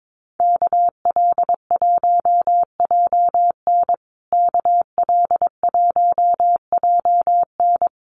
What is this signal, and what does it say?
Morse code, heard as a keyed tone, about 22 wpm, 700 Hz. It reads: KL1JN KL1JN
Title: USA Call Signs Repeated 3x 22wpm